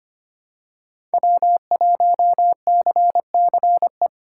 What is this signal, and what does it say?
Morse code, heard as a keyed tone, about 25 wpm, 700 Hz. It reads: W1CCE
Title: K1USN SST QSO Practice - 25wpm